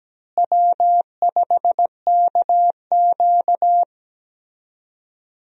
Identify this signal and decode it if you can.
Morse code, heard as a keyed tone, about 17 wpm, 700 Hz. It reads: W5KQ
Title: USA Call Signs - 4 Characters - Repeated 3x - 17wpm